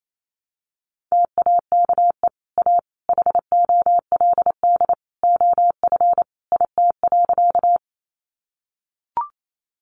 Morse code, heard as a keyed tone, about 28 wpm, 700 Hz. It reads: TAKE A HOLD OF IT.
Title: Sentences from Top 400 Words 28wpm